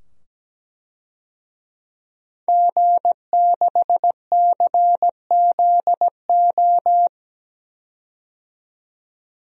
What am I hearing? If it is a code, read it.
Morse code, heard as a keyed tone, about 17 wpm, 700 Hz. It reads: G6CZO
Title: DXCC-100 Call Signs - 5 Characters - 17wpm